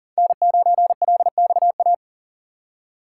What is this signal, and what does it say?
Morse code, heard as a keyed tone, about 40 wpm, 700 Hz. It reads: N9LXA
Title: USA Call Signs - 5 Characters - 4x Character Spacing - 40wpm